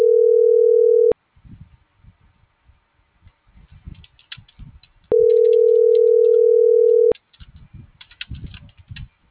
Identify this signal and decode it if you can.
Ringback tone (North America), 440 + 480 Hz.
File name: CENTRAL FRALSE / out gerardo2024sub